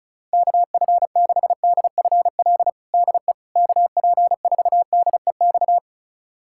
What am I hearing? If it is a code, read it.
Morse code, heard as a keyed tone, about 36 wpm, 700 Hz. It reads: KF6DFL DE KP4DEX
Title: USA Callsign DE Callsign - 6 Characters - 35wpm